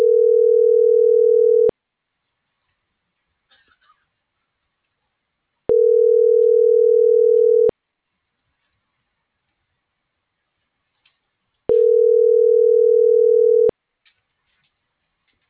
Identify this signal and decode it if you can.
Ringback tone (North America), 440 + 480 Hz.